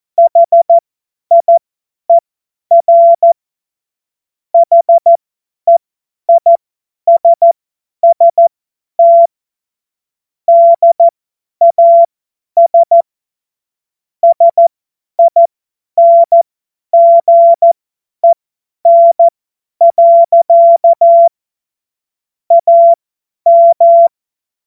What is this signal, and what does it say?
Morse code, heard as a keyed tone, about 14 wpm, spaced as for 10 wpm, 680 Hz. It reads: HIER HEISST DAS SINGEN. AM
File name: Morsekurs / Witze / Witz 12. Oktober 2021, 50 F70